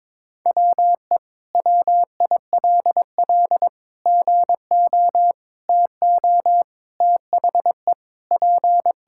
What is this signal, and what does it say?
Morse code, heard as a keyed tone, about 22 wpm, 700 Hz. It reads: WE WILL GO TO THE P
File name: practice / 22wpm / Sentences with 2-Word Phrases - 22wpm